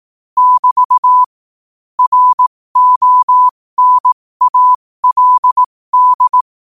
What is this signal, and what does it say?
Morse code, heard as a keyed tone, about 18 wpm, spaced as for 15 wpm, 1000 Hz. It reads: <BT> RONALD
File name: qotdfiles / 15wpm